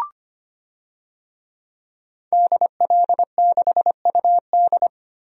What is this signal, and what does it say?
Morse code, heard as a keyed tone, about 25 wpm, 700 Hz. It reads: DL6UD
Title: DXCC-100 Call Signs - 5 Characters - Repeated 3x - 25wpm